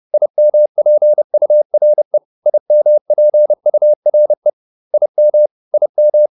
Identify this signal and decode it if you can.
Morse code, heard as a keyed tone, about 30 wpm, 600 Hz. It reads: IMPURE IMPURE IM IM
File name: IM6 30